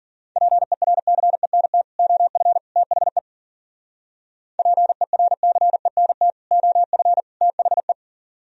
Morse code, heard as a keyed tone, about 51 wpm, 700 Hz. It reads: PERCENT OF THE PERCENT OF THE